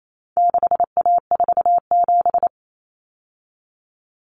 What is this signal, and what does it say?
Morse code, heard as a keyed tone, about 28 wpm, 700 Hz. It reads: 6A47
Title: Common Cut Numbers - 4-Digit 28wpm